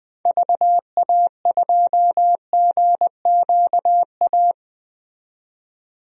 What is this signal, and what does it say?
Morse code, heard as a keyed tone, about 21 wpm, 700 Hz. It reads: VA2GQA
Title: DXCC-100 Call Signs - 6 Characters - 20wpm